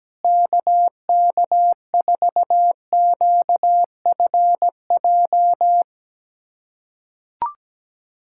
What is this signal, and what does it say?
Morse code, heard as a keyed tone, about 17 wpm, 700 Hz. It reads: KK4QFJ
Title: USA Callsign DE Callsign - 6 Characters - 17wpm